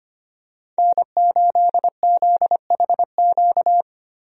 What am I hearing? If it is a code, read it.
Morse code, heard as a keyed tone, about 25 wpm, 700 Hz. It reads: N8ZHQ
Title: USA Call Signs - 5 Characters - 25wpm